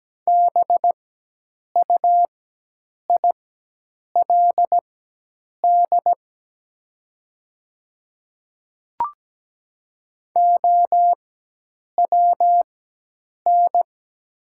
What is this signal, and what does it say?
Morse code, heard as a keyed tone, about 17 wpm, spaced as for 8 wpm, 700 Hz. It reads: BUILD OWN
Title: 4x Character Spacing - Top 500 Words - 17wpm